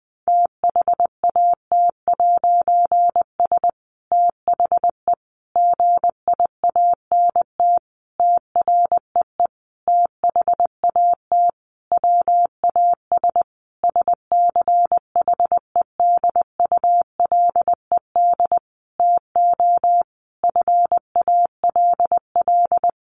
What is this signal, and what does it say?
Morse code, heard as a keyed tone, about 20 wpm, 700 Hz. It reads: THAT'S THE GIANT TREE THAT WAS SCHEDULED TO FALL
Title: A Sound Of Thunder 20wpm